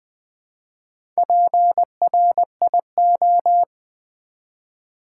Morse code, heard as a keyed tone, about 20 wpm, 700 Hz. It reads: PRIO
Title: Build A Word - 1 Letter At A Time - 3 to 5-Letter Words - 20wpm